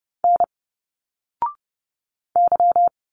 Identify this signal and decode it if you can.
Morse code, heard as a keyed tone, about 30 wpm, 700 Hz. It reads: N Y
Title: Single Letters 30wpm